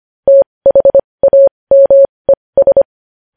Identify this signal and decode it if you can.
Morse code, heard as a keyed tone, about 25 wpm, spaced as for 20 wpm, 550 Hz. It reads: THAMES
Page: news-headlines-in-morse-code-at-20-wpm